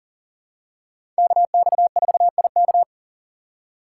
Morse code, heard as a keyed tone, about 40 wpm, 700 Hz. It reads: KX4IK
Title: USA Call Signs - 5 Characters - 2x Character Spacing - 40wpm